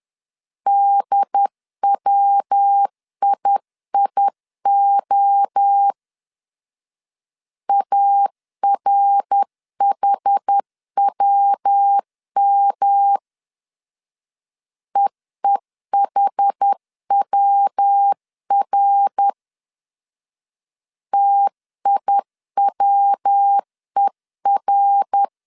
Morse code, heard as a keyed tone, about 10 wpm, 800 Hz. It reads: DWIIO ARHWM EEHWR TIWER
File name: 10wpm